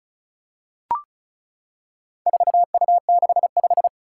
Morse code, heard as a keyed tone, about 36 wpm, 700 Hz. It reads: 4U65
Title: Four Challenging Letter-Number Combinations 35wpm